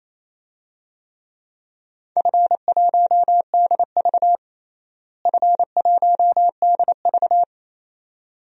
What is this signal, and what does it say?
Morse code, heard as a keyed tone, about 28 wpm, 700 Hz. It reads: F1DV F1DV
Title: DXCC-100 Call Signs - 4 Characters - Repeated 2x - 28wpm